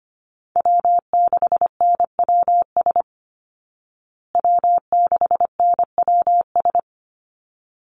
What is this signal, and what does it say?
Morse code, heard as a keyed tone, about 25 wpm, 700 Hz. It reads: W6NWS W6NWS
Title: Call Signs Repeated 3x - 25wpm